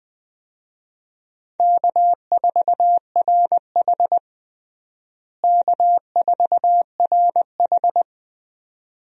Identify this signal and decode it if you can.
Morse code, heard as a keyed tone, about 20 wpm, 700 Hz. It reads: K4RH K4RH
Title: USA Call Signs - 4 Characters - Repeated 2x - 20wpm